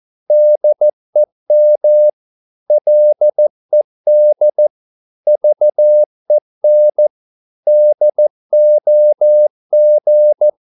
Morse code, heard as a keyed tone, about 14 wpm, 600 Hz. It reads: DEM LED VEN DOG
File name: Korte ord DK 14wpm